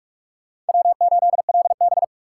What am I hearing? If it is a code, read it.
Morse code, heard as a keyed tone, about 42 wpm, 700 Hz. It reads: W8LB